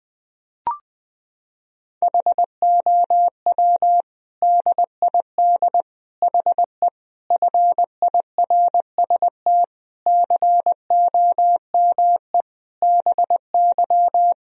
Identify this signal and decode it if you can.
Morse code, heard as a keyed tone, about 21 wpm, 700 Hz. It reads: HOW DID HE FIRST COME BY
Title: Sentences from Top 1000 Words - Review Part 2 - 20wpm